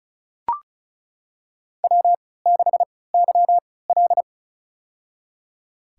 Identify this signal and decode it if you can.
Morse code, heard as a keyed tone, about 34 wpm, spaced as for 20 wpm, 690 Hz. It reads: W6YL
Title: USA Call Signs - 4 Characters - 3x Character Spacing - 35wpm